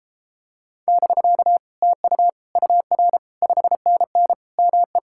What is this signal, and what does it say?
Morse code, heard as a keyed tone, about 33 wpm, 700 Hz. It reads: <BK> TU UR 5NN ME